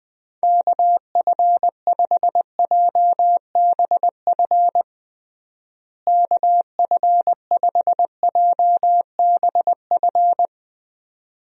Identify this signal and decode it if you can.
Morse code, heard as a keyed tone, about 20 wpm, 700 Hz. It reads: KF5JBF KF5JBF
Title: USA Call Signs - 6 Characters - Repeated 2x - 20wpm